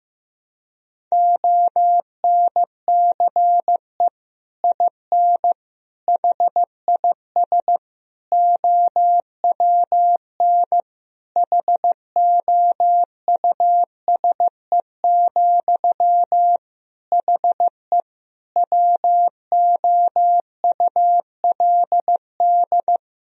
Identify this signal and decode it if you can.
Morse code, heard as a keyed tone, about 15 wpm, 700 Hz. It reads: ONCE IN HIS OWN HOUSE, HE WOULD
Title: Sentences from Top 1000 Words - Review Part 2 - 15wpm